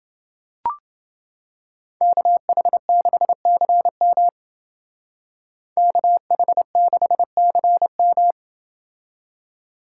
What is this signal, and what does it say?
Morse code, heard as a keyed tone, about 29 wpm, 700 Hz. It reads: KH6CM KH6CM